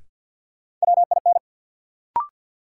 Morse code, heard as a keyed tone, about 52 wpm, 700 Hz. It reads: WIN
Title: Sentences from Top 1500 Words - New Word Drill Alt - 50wpm